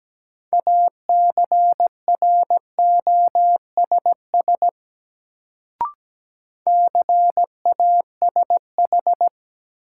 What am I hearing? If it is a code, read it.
Morse code, heard as a keyed tone, about 17 wpm, 700 Hz. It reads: ACROSS CASH
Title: Sentences from Top 700 Words - New Word Drill - 17wpm